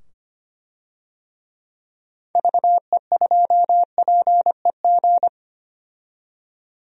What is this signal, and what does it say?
Morse code, heard as a keyed tone, about 25 wpm, 700 Hz. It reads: VE2PEG
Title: DXCC-100 Call Signs - 6 Characters - 25wpm